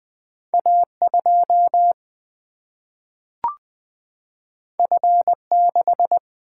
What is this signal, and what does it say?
Morse code, heard as a keyed tone, about 20 wpm, 700 Hz. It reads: A 2 F 6
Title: Single-Letter Single-Number 20wpm